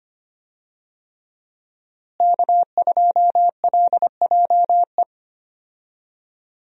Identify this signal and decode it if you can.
Morse code, heard as a keyed tone, about 25 wpm, 700 Hz. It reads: K2LJE
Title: USA Call Signs - 5 Characters - Repeated 3x - 25wpm